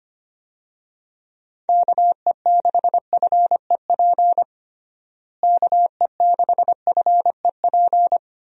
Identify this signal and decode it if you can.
Morse code, heard as a keyed tone, about 25 wpm, 700 Hz. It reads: KE6FEP KE6FEP
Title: USA Call Signs - 6 Characters - Repeated 2x - 25wpm